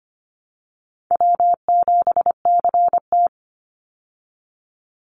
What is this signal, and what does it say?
Morse code, heard as a keyed tone, about 25 wpm, 700 Hz. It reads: W7CT
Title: Easy Call Signs Repeated 3x - 25wpm